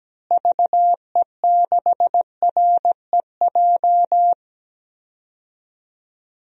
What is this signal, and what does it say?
Morse code, heard as a keyed tone, about 17 wpm, 700 Hz. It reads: VE6REJ
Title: DXCC-100 Call Signs - 6 Characters - Repeated 3x - 17wpm